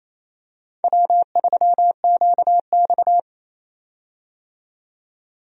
Morse code, heard as a keyed tone, about 27 wpm, 700 Hz. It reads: W3QX
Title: USA Call Signs - 4 Characters - 28wpm